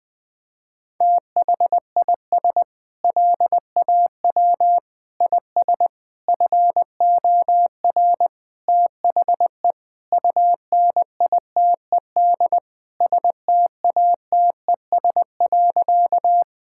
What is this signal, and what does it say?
Morse code, heard as a keyed tone, about 20 wpm, 700 Hz. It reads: THIS LAW IS FOR THE UNITED STATES.
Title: Sentences with 4-Word Phrases - 20wpm